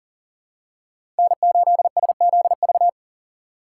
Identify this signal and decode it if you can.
Morse code, heard as a keyed tone, about 40 wpm, 700 Hz. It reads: N8SZV
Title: USA Call Signs - 5 Characters - 3x Character Spacing - 40wpm